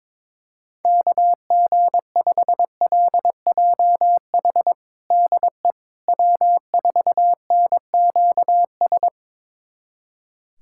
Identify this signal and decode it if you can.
Morse code, heard as a keyed tone, about 22 wpm, 700 Hz. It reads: KG5LJH DE W4NQS
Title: USA Callsign DE Callsign 22wpm